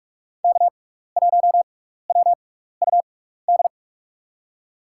Morse code, heard as a keyed tone, about 44 wpm, spaced as for 15 wpm, 700 Hz. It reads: K1WUD
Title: USA Call Signs - 6x Character Spacing - 45wpm